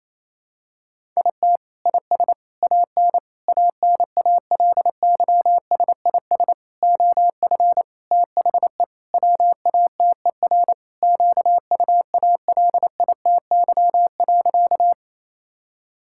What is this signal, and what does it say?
Morse code, heard as a keyed tone, about 28 wpm, 700 Hz. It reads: IT IS AN ANALYSIS OF THE WATER QUALITY.
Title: Sentences from Top 1000 Words - Review Part 5 - 28wpm